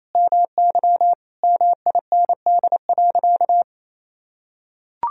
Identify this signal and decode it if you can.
Morse code, heard as a keyed tone, about 28 wpm, 700 Hz. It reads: MY MIND.
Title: Sentences from Top 1000 Words - Review Part 1 - 28wpm